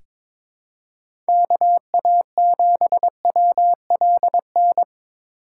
Morse code, heard as a keyed tone, about 22 wpm, 700 Hz. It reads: KA7WLN
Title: USA Call Signs - 6 Characters - 2x Character Spacing - 22wpm